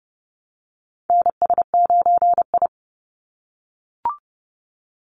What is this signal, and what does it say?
Morse code, heard as a keyed tone, about 30 wpm, 700 Hz. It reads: NS9I
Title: Call Signs - 30wpm